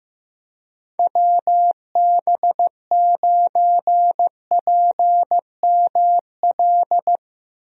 Morse code, heard as a keyed tone, about 15 wpm, 700 Hz. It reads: WB9PML
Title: USA Call Signs - 6 Characters - Repeated 2x - 15wpm